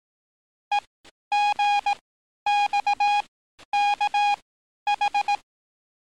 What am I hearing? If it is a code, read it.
Morse code, heard as a keyed tone, about 18 wpm, spaced as for 11 wpm, 820 Hz. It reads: EGXKH
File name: training / NAVPERS 11049to11060 / NAVPERS 11050 RA RB